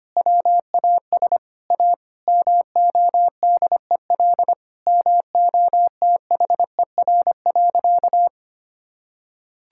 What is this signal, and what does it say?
Morse code, heard as a keyed tone, about 25 wpm, 700 Hz. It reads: WAS A MODEL MOTHER.
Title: Sentences from Top 1000 Words - Review Part 5 - 25wpm